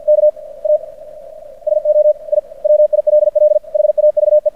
Morse code, heard as a keyed tone, about 50 wpm, 600 Hz. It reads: K A SO I WILL HAVE
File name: archived / CRSnet